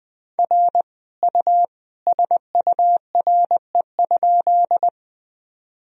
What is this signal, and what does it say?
Morse code, heard as a keyed tone, about 20 wpm, 700 Hz. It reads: R U SURE?
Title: Sentences from Top 300 Words and Abbreviations 20wpm